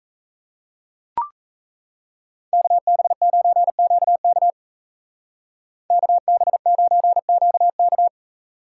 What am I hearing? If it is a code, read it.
Morse code, heard as a keyed tone, about 33 wpm, 700 Hz. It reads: KB9QK KB9QK
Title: Speed Racing - USA Call Signs 28wpm